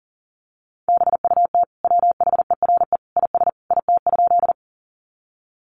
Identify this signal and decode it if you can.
Morse code, heard as a keyed tone, about 40 wpm, 700 Hz. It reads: BUT WHERE IS IT?
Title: Sentences from Top 200 Words 40wpm